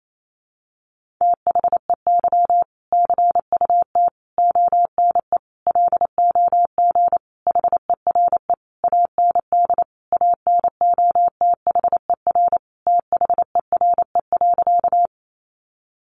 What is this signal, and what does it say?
Morse code, heard as a keyed tone, about 28 wpm, 700 Hz. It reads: THEY CUT ONE LOG HERE AND ANOTHER THERE.
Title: Sentences from Top 1400 Words 28wpm